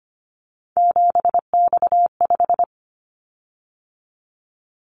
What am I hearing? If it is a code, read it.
Morse code, heard as a keyed tone, about 25 wpm, 700 Hz. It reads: 7X5
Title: Callsign Prefixes for Top 100 DXCC - 25wpm